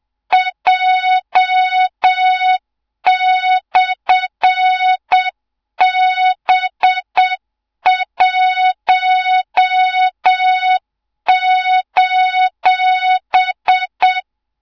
Morse code, heard as a keyed tone, about 7 wpm, 3700 Hz, with noise. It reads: J/B1:
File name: Morze-HA2MN / hang / 24 GYAKORLAT